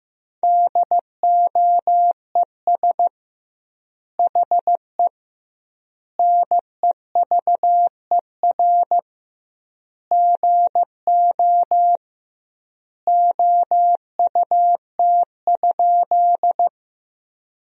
Morse code, heard as a keyed tone, about 16 wpm, 700 Hz. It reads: DOES HE NEVER GO OUT?
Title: 2x Word-Spacing - Sentences from Top 300 Words - 15wpm